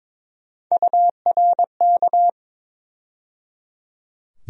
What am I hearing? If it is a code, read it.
Morse code, heard as a keyed tone, about 21 wpm, 700 Hz. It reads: URK
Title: USA Call Sign Postfixes - 3 Letters 22wpm